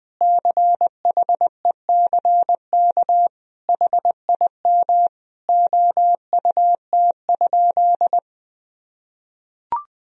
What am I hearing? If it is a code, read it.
Morse code, heard as a keyed tone, about 20 wpm, 700 Hz. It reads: CHECK HIM OUT?
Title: Sentences from Top 1000 Words - Review Part 3 - 20wpm